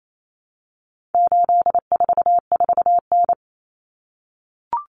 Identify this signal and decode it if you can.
Morse code, heard as a keyed tone, about 28 wpm, 700 Hz. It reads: 844N
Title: Common Cut Numbers - 4-Digit 28wpm